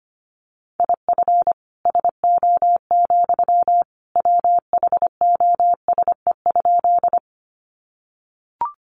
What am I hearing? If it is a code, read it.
Morse code, heard as a keyed tone, about 25 wpm, 700 Hz. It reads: IF SO, WHOSE?
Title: Sentences from Top 1200 Words 25wpm